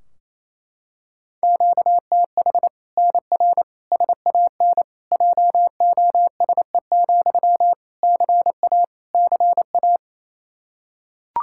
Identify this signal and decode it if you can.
Morse code, heard as a keyed tone, about 28 wpm, 700 Hz. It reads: QTH NR SAN JOSE, CA CA
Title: Practice QSO - Exchange 1 - Beginning - 28wpm